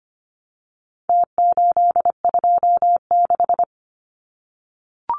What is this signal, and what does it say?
Morse code, heard as a keyed tone, about 25 wpm, 700 Hz. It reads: T826
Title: Numbers - Contest Style - 4-Digit 25wpm